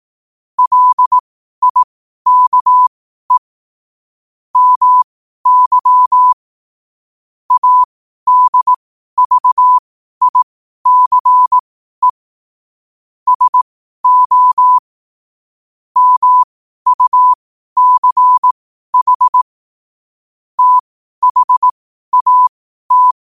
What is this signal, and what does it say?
Morse code, heard as a keyed tone, about 18 wpm, spaced as for 12 wpm, 1000 Hz. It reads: LIKE MY ADVICE SO MUCH THAT
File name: qotdfiles / 12wpm